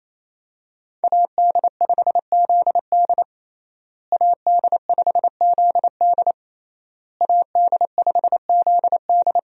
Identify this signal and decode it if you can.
Morse code, heard as a keyed tone, about 28 wpm, 700 Hz. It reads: AD5ZD AD5ZD AD5ZD
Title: USA Call Signs - 5 Characters - Repeated 3x - 28wpm